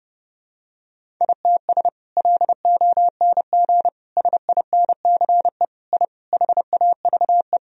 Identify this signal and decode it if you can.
Morse code, heard as a keyed tone, about 30 wpm, 700 Hz. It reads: ITS LONG SINCE I HAVE
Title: Sentences from Top 1000 Words - Review Part 1 - 30wpm